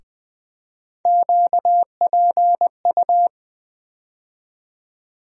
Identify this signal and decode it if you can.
Morse code, heard as a keyed tone, about 20 wpm, 700 Hz. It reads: QPU
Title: USA Call Sign Postfixes - 3 Letters 20wpm